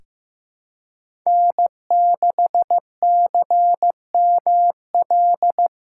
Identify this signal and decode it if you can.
Morse code, heard as a keyed tone, about 15 wpm, 700 Hz. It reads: N6CML
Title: USA Call Signs - 4x Character Spacing - 15wpm